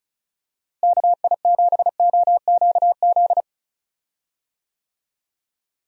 Morse code, heard as a keyed tone, about 36 wpm, 700 Hz. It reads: KI7OQZ
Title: USA Call Signs - 6 Characters - Repeated 3x - 35wpm